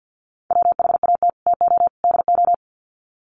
Morse code, heard as a keyed tone, about 50 wpm, 700 Hz. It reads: WHAT TO DO
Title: Sets of 3 Words 50wpm